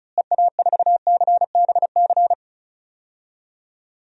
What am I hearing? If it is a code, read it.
Morse code, heard as a keyed tone, about 34 wpm, 700 Hz. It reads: EA4CBC
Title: DXCC-100 Call Signs - 6 Characters - Repeated 2x - 35wpm